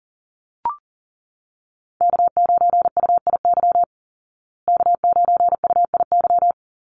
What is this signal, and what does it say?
Morse code, heard as a keyed tone, about 40 wpm, 700 Hz. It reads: K9UIY K9UIY
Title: Call Signs Repeated 2x - 40wpm